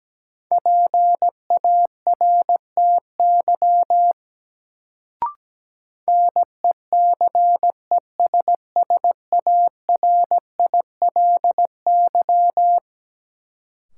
Morse code, heard as a keyed tone, about 17 wpm, 700 Hz. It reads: PARTY NECESSARILY
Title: Sentences from Top 1400 Words - New Word Drill - 17wpm